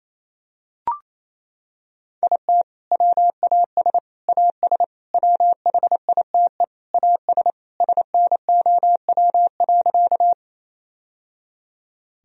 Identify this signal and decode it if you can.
Morse code, heard as a keyed tone, about 28 wpm, 700 Hz. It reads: IT WAS AS WHITE AS SNOW.
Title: Sentences from Top 1000 Words - Review Part 3 - 28wpm